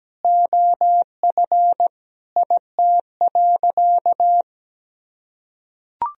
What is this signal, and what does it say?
Morse code, heard as a keyed tone, about 17 wpm, 700 Hz. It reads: OF IT.
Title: Sentences from Top 1000 Words - Review Part 3 - 17wpm